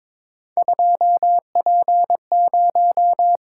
Morse code, heard as a keyed tone, about 22 wpm, 700 Hz. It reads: 2P0
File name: practice / 22wpm / Three Letter-Number Combinations - 22wpm